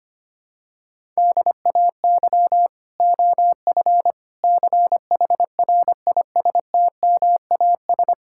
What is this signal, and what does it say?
Morse code, heard as a keyed tone, about 25 wpm, 700 Hz. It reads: DAY OF CHRISTMAS
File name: Sets of 9 Words - 25wpm